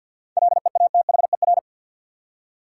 Morse code, heard as a keyed tone, about 52 wpm, 700 Hz. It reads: LEATHER
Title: Seven Letter Words - 50wpm